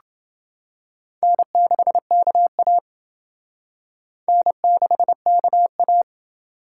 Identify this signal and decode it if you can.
Morse code, heard as a keyed tone, about 28 wpm, 700 Hz. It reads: N6KA N6KA